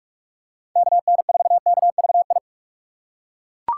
Morse code, heard as a keyed tone, about 40 wpm, 700 Hz. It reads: KN4KVI